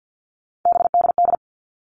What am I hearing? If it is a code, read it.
Morse code, heard as a keyed tone, about 50 wpm, 700 Hz. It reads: BDD
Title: All Cut Numbers - 3-Digit 50wpm